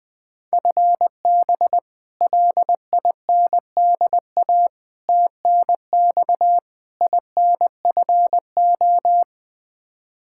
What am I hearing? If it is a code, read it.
Morse code, heard as a keyed tone, about 21 wpm, 700 Hz. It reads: FB LINDA TNX INFO
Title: Practice QSO - Exchange 3 - More Info - 20wpm